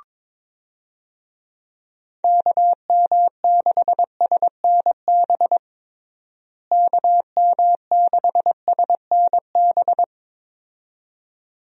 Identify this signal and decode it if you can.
Morse code, heard as a keyed tone, about 22 wpm, 700 Hz. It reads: KM6SNB KM6SNB
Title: USA Call Signs - 6 Characters - Repeated 2x - 22wpm